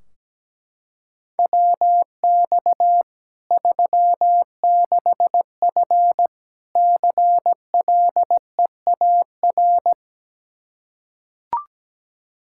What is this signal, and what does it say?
Morse code, heard as a keyed tone, about 17 wpm, 700 Hz. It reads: WX 36F CLEAR
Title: Practice QSO - Exchange 2 - Info - 17wpm